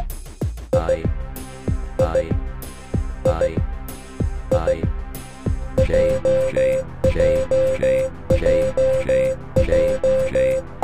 Morse code, heard as a keyed tone, about 16 wpm, 520 Hz. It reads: I I I I JJJJ